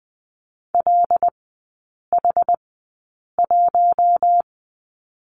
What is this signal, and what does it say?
Morse code, heard as a keyed tone, about 20 wpm, spaced as for 8 wpm, 700 Hz. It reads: LH1
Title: 191 Lesson 31e - Letter v - Three Characters 20wpm